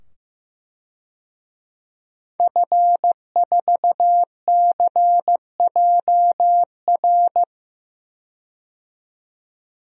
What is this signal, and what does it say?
Morse code, heard as a keyed tone, about 15 wpm, 700 Hz. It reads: F4CJR
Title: DXCC-100 Call Signs - 5 Characters - 15wpm